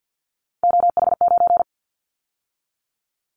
Morse code, heard as a keyed tone, about 50 wpm, 700 Hz. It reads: OH9
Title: Callsign Prefixes for Top 50 DXCC - 50wpm